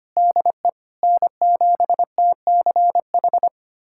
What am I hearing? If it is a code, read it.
Morse code, heard as a keyed tone, about 25 wpm, 700 Hz. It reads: DE N7TCH
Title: USA Callsign DE Callsign - 5 Characters - 25wpm